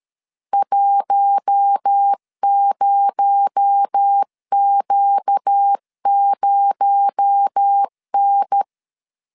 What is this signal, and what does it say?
Morse code, heard as a keyed tone, about 13 wpm, 800 Hz. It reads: 10Q0N
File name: CW / DET / 1 6 1500C S A 12wpm 3440